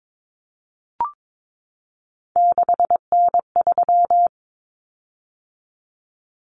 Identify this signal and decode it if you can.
Morse code, heard as a keyed tone, about 22 wpm, 700 Hz. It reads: 6N3
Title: Common Cut Numbers - 3-Digit 22wpm